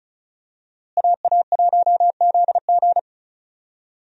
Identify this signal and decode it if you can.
Morse code, heard as a keyed tone, about 36 wpm, 700 Hz. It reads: AA1ZG
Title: USA Call Signs - 5 Characters - Repeated 3x - 35wpm